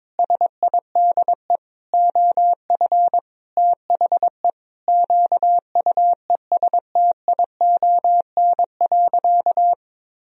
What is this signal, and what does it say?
Morse code, heard as a keyed tone, about 22 wpm, 700 Hz. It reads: SIDE OF THE QUESTION.
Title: Sentences from Top 1000 Words - Review Part 1 - 22wpm